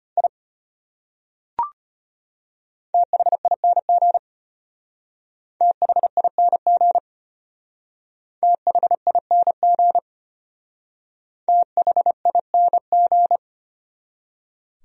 Morse code, heard as a keyed tone, about 31 wpm, 700 Hz. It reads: I THING THING THING THING